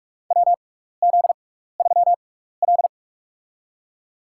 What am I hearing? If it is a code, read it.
Morse code, heard as a keyed tone, about 47 wpm, spaced as for 16 wpm, 700 Hz. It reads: WZ3L
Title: USA Call Signs - 4 Characters - 6x Character Spacing - 45wpm